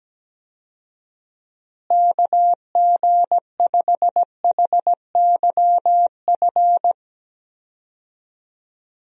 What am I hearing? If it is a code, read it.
Morse code, heard as a keyed tone, about 17 wpm, 700 Hz. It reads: KG5HYF
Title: USA Call Signs - 6 Characters - Repeated 3x - 17wpm